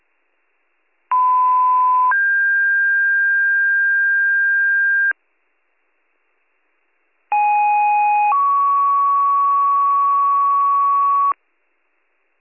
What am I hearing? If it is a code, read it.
Two-tone page, heard as a one-second tone, then a three-second tone, about 1010 then 1600 Hz, about 830 then 1120 Hz. It is heard from 1.1 to 11.3 s.